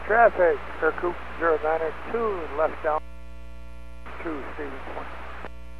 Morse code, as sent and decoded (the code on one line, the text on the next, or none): none